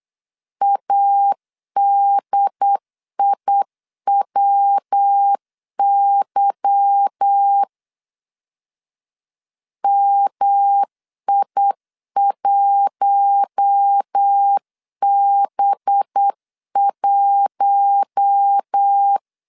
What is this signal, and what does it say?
Morse code, heard as a keyed tone, about 8 wpm, 800 Hz. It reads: ADIWY MI1B1
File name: CW / DET / 1 9 1500C S A 8wpm 5637